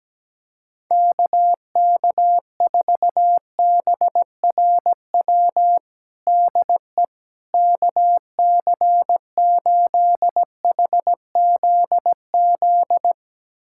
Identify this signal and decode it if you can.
Morse code, heard as a keyed tone, about 17 wpm, 700 Hz. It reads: KK4BRW DE KC8HZZ
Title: USA Callsign DE Callsign - 6 Characters - 17wpm